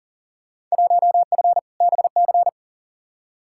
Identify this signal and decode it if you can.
Morse code, heard as a keyed tone, about 40 wpm, 700 Hz. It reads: 1F BC
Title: Field Day Exchanges Encore - 40wpm